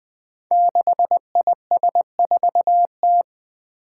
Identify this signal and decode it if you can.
Morse code, heard as a keyed tone, about 20 wpm, 700 Hz. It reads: 6IS4T
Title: Five Challenging Letter-Number Combinations 20wpm